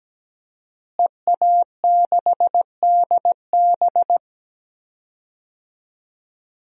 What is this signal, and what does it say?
Morse code, heard as a keyed tone, about 17 wpm, 700 Hz. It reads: EA6DB
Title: DXCC-100 Call Signs - 5 Characters - Repeated 3x - 17wpm